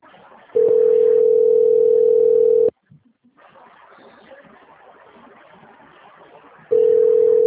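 Ringback tone (North America), 440 + 480 Hz.
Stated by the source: Presidente Prudente - SP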